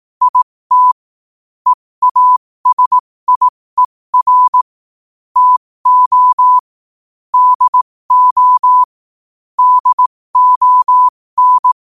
Morse code, heard as a keyed tone, about 18 wpm, spaced as for 15 wpm, 1000 Hz. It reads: IT EASIER TO DO DON
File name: qotdfiles / 15wpm